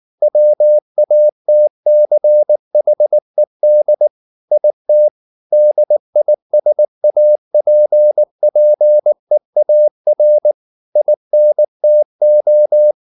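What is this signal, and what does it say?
Morse code, heard as a keyed tone, about 19 wpm, 600 Hz. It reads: WATCHED IT DISAPPEAR INTO
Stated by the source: Morse code at 19 wpm from CW med Gnister, reading Ken Follett's novel Never in English.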